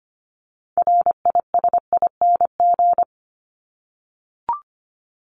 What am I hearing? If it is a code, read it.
Morse code, heard as a keyed tone, about 25 wpm, 700 Hz. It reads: RISING
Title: ING Suffix - Top 100 Words with Base Words 25wpm